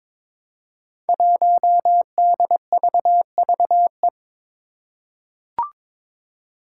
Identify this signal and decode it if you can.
Morse code, heard as a keyed tone, about 22 wpm, 700 Hz. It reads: 1DVVE
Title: Five Challenging Letter-Number Combinations 22wpm